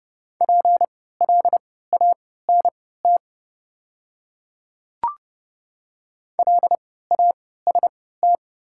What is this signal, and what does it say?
Morse code, heard as a keyed tone, about 30 wpm, spaced as for 17 wpm, 700 Hz. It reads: PLANT LAST
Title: 3x Character Spacing - Top 500 Words - 30wpm